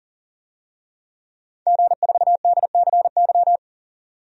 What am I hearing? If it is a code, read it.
Morse code, heard as a keyed tone, about 40 wpm, 700 Hz. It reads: G4DCY
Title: DXCC-100 Call Signs - 5 Characters - 40wpm